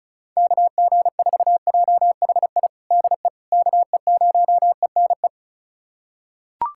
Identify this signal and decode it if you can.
Morse code, heard as a keyed tone, about 36 wpm, 700 Hz. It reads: KG4JHI DE KE0ENE
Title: USA Callsign DE Callsign 35wpm